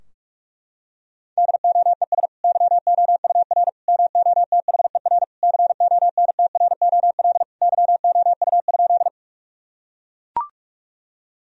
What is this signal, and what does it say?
Morse code, heard as a keyed tone, about 41 wpm, 700 Hz. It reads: DOES YOUR MOTHER CONTROL YOU?
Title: Sentences from Top 1000 Words - Review Part 3 - 45wpm